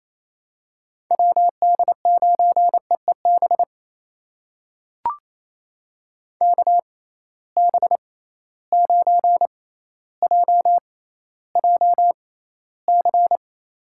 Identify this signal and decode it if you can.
Morse code, heard as a keyed tone, about 28 wpm, spaced as for 14 wpm, 700 Hz. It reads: WD9EEB KB9JJC
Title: USA Call Signs - 6 Characters - 6x Character Spacing - 28wpm